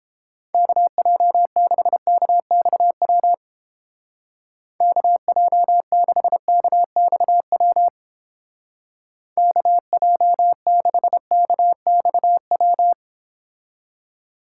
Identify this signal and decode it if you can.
Morse code, heard as a keyed tone, about 28 wpm, 700 Hz. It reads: KJ6KXW KJ6KXW KJ6KXW